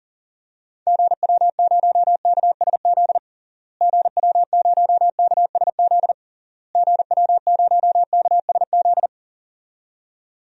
Morse code, heard as a keyed tone, about 40 wpm, 700 Hz. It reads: GW0KSZ GW0KSZ GW0KSZ
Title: DXCC-100 Call Signs - 6 Characters - Repeated 3x - 40wpm